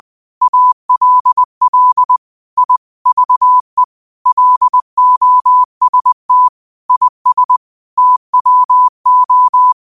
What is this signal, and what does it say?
Morse code, heard as a keyed tone, about 20 wpm, 1000 Hz. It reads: ALL I VE LOST IS TWO
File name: qotdfiles / 20wpm